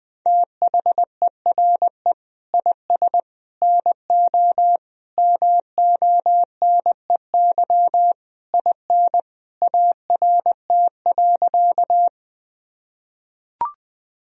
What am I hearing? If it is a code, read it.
Morse code, heard as a keyed tone, about 20 wpm, 700 Hz. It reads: THERE IS NO MONEY IN ART.
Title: Sentences from Top 1000 Words - Review Part 3 - 20wpm